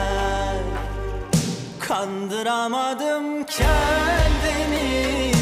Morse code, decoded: none